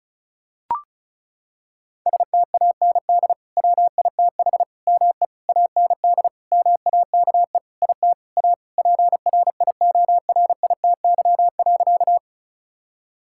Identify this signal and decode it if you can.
Morse code, heard as a keyed tone, about 34 wpm, 700 Hz. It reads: STAND WITH ME AND MAKE IT A PRIORITY.
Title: Sentences from Top 1600 Words 35wpm